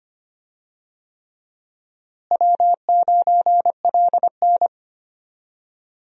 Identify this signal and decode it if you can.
Morse code, heard as a keyed tone, about 25 wpm, 700 Hz. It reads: W9LN
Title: USA Call Signs - 4 Characters - Repeated 3x - 25wpm